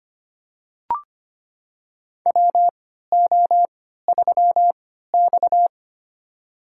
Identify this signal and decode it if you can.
Morse code, heard as a keyed tone, about 25 wpm, spaced as for 14 wpm, 700 Hz. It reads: WO3X
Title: USA Call Signs - 4 Characters - 3x Character Spacing - 25wpm